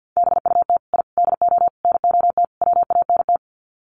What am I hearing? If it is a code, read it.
Morse code, heard as a keyed tone, about 50 wpm, 700 Hz. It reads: BUT I DO NOT WANT
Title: Sets of 5 Words 50wpm